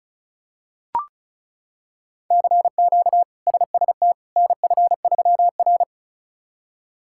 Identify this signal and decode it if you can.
Morse code, heard as a keyed tone, about 36 wpm, 700 Hz. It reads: CQ SST NF3R
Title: K1USN SST QSO Practice - 35wpm